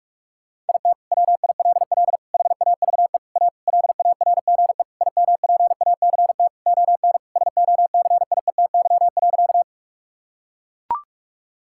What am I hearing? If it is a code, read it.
Morse code, heard as a keyed tone, about 42 wpm, 700 Hz. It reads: IT WILL HAVE A LARGE IMPACT ON SOCIETY.
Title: Sentences from Top 1000 Words - Review Part 6 - 45wpm